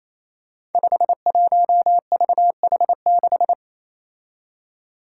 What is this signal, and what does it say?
Morse code, heard as a keyed tone, about 27 wpm, 700 Hz. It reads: 51VH6
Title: Five Challenging Letter-Number Combinations 28wpm